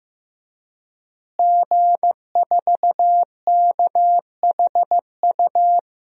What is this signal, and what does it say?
Morse code, heard as a keyed tone, about 15 wpm, 700 Hz. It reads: G4KHU
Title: DXCC-100 Call Signs - 5 Characters - Repeated 3x - 15wpm